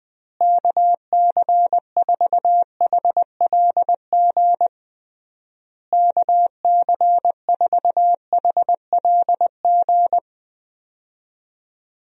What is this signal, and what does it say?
Morse code, heard as a keyed tone, about 20 wpm, 700 Hz. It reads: KC4HLG KC4HLG
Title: USA Call Signs - 6 Characters - Repeated 3x - 20wpm